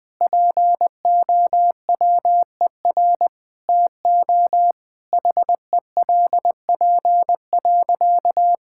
Morse code, heard as a keyed tone, about 20 wpm, 700 Hz. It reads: POWER TO HELP.
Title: Sentences with 8-Word Phrases - 20wpm